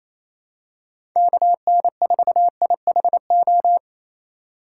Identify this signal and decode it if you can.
Morse code, heard as a keyed tone, about 28 wpm, 700 Hz. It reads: KN4IHO
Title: USA Call Signs - 6 Characters - 2x Character Spacing - 28wpm